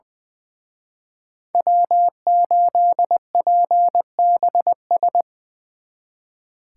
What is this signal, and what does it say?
Morse code, heard as a keyed tone, about 23 wpm, 700 Hz. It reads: W8PBS
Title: Speed Racing - USA 1x3 Call Signs 20wpm